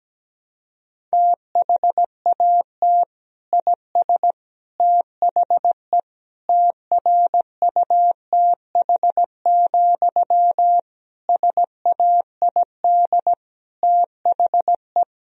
Morse code, heard as a keyed tone, about 17 wpm, 700 Hz. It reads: THAT IS THE TRUTH, SAID THE
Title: Sentences from Top 1000 Words - Review Part 1 - 17wpm